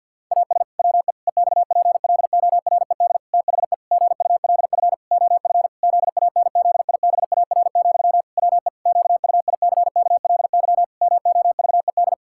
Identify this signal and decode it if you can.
Morse code, heard as a keyed tone, about 50 wpm, 700 Hz. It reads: AS WE EXPLORED THE GULF OF ZANZIBAR, WE QUICKLY MOVED
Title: Pangram Challenge v2 - 50wpm